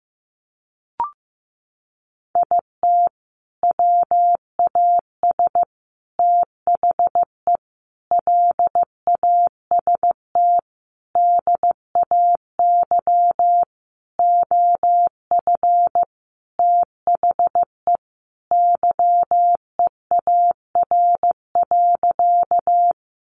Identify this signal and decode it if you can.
Morse code, heard as a keyed tone, about 15 wpm, 700 Hz. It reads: IT WAS THE LAST DAY OF THE YEAR.
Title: Sentences from Top 300 Words 15wpm